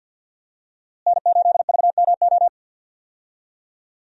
Morse code, heard as a keyed tone, about 53 wpm, 700 Hz. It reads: N8VMO
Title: USA Call Signs - 5 Characters - Repeated 3x - 50wpm